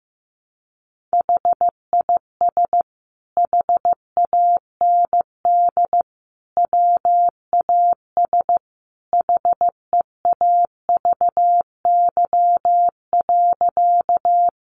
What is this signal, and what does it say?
Morse code, heard as a keyed tone, about 15 wpm, 700 Hz. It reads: HIS HAND WAS HEAVY.
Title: Sentences from Top 1100 Words 15wpm